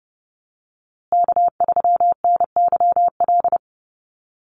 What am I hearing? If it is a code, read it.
Morse code, heard as a keyed tone, about 30 wpm, 700 Hz. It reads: K3NYL
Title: USA Call Signs Repeated 2x 30wpm